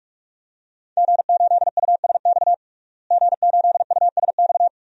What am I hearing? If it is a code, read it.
Morse code, heard as a keyed tone, about 46 wpm, 700 Hz. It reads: G8USX G8USX
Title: DXCC-100 Call Signs - 5 Characters - Repeated 3x - 45wpm